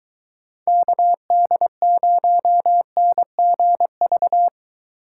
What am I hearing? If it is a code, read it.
Morse code, heard as a keyed tone, about 21 wpm, 700 Hz. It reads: KD0NGV